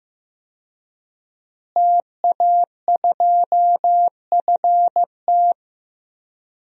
Morse code, heard as a keyed tone, about 15 wpm, 700 Hz. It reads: TA2FT
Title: DXCC-100 Call Signs - 5 Characters - Repeated 2x - 15wpm